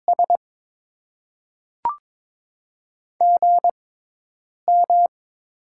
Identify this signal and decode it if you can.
Morse code, heard as a keyed tone, about 22 wpm, spaced as for 8 wpm, 700 Hz. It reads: S GM
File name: Top 20 QSO Elements - 6x Character Spacing - 22wpm